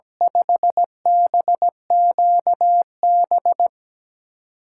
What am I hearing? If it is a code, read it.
Morse code, heard as a keyed tone, about 17 wpm, 700 Hz. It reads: 5BQB
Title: USA Call Signs - 6 Characters - 8x Character Spacing - 17wpm